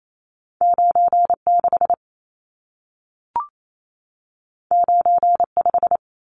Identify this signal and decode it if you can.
Morse code, heard as a keyed tone, about 28 wpm, 700 Hz. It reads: 96 95
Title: Numbers - 2-Digit 28wpm